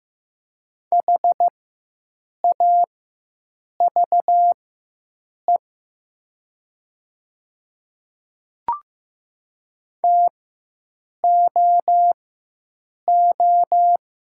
Morse code, heard as a keyed tone, about 15 wpm, spaced as for 7 wpm, 700 Hz. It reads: HAVE TOO
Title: 4x Character Spacing - Top 200 Words - 15wpm